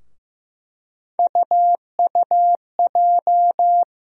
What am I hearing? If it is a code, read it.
Morse code, heard as a keyed tone, about 15 wpm, 700 Hz. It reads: UUJ
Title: USA Call Sign Postfixes - 3 Letters 15wpm